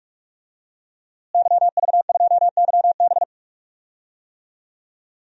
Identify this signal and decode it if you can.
Morse code, heard as a keyed tone, about 46 wpm, 700 Hz. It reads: YV2YB
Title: DXCC-100 Call Signs - 5 Characters - Repeated 3x - 45wpm